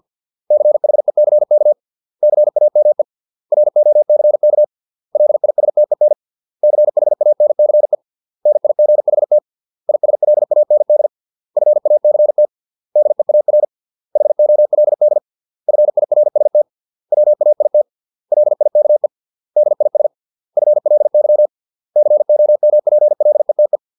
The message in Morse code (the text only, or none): CHECK CAME ROCK LISTEN CHANCE NIGHT ISLAND FACT DEAR HOLD FIRST WAIT LIKE DIS FLY COMPLETE